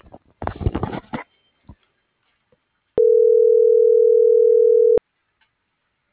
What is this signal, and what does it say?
Ringback tone (North America), 440 + 480 Hz.